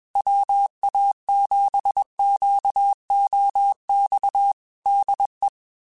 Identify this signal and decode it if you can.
Morse code, heard as a keyed tone, about 21 wpm, 800 Hz. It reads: WA7QOX DE
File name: QSOs / QSOs-mp3s / WA7QOX 1@20